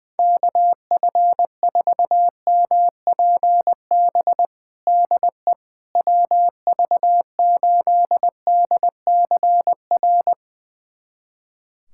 Morse code, heard as a keyed tone, about 20 wpm, 690 Hz. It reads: KF4MPB DE WV8DCR
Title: USA Callsign DE Callsign 20wpm